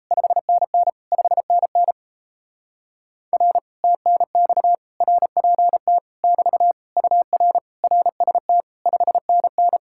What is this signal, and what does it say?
Morse code, heard as a keyed tone, about 38 wpm, 700 Hz. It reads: 5NN 5NN R TNX RPT <BT> UR RST 5NN